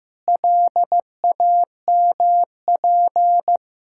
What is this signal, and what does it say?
Morse code, heard as a keyed tone, about 16 wpm, 700 Hz. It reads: LAMP
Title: Pangram Challenge v2 - 15wpm